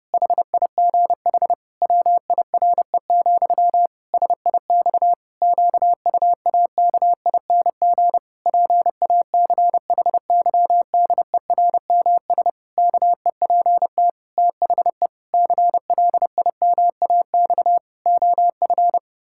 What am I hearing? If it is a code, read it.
Morse code, heard as a keyed tone, about 30 wpm, 700 Hz. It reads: HIGH WIRE, SIX QUAKING PACHYDERMS KEPT THE CLIMAX OF
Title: Pangram Challenge v2 - 30wpm